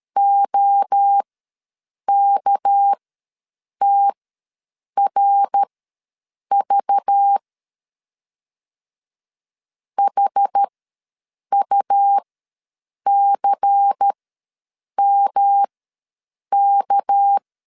Morse code, heard as a keyed tone, about 13 wpm, spaced as for 7 wpm, 800 Hz. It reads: OKTRV HUCMK
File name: LEZIONI / 6 PKX0 / 6 PK ws 12wpm 3116